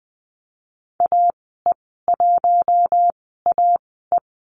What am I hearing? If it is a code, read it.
Morse code, heard as a keyed tone, about 20 wpm, spaced as for 15 wpm, 700 Hz. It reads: AE1AE
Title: 076 Lesson 13g - Letter L - Call Signs 20wpm